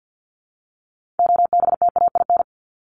Morse code, heard as a keyed tone, about 50 wpm, 700 Hz. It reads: OBTAIN
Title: Top 1001-1250 Words 50wpm